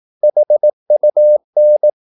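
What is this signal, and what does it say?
Morse code, heard as a keyed tone, about 18 wpm, 600 Hz. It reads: HUN
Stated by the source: Slow (QRS) Morse practice at 18 wpm, CW med Gnister.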